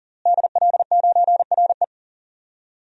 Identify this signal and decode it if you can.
Morse code, heard as a keyed tone, about 38 wpm, 700 Hz. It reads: DL9RE
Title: DXCC-100 Call Signs - 5 Characters - Repeated 2x - 40wpm